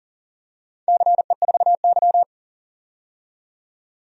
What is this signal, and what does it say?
Morse code, heard as a keyed tone, about 40 wpm, 700 Hz. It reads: CE4Y
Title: DXCC-100 Call Signs - 4 Characters - 40wpm